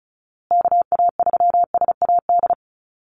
Morse code, heard as a keyed tone, about 34 wpm, 700 Hz. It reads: KA3SAD
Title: USA Call Signs Repeated 2x 35wpm